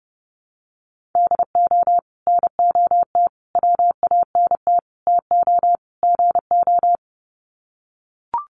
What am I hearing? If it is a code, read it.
Morse code, heard as a keyed tone, about 30 wpm, 700 Hz. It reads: DO NOT WANT TO GO
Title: Sets of 5 Words 30wpm